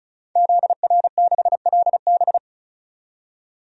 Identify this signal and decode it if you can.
Morse code, heard as a keyed tone, about 36 wpm, 700 Hz. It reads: ZR6LB
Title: DXCC-100 Call Signs - 5 Characters - 35wpm